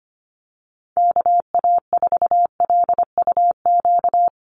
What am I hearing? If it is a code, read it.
Morse code, heard as a keyed tone, about 25 wpm, 700 Hz. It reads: KA4LUQ
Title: USA Call Signs 25wpm